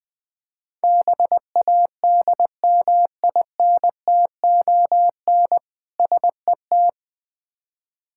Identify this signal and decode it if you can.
Morse code, heard as a keyed tone, about 21 wpm, 690 Hz. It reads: BADMINTON SET
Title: Difficult Words with Hints - 20wpm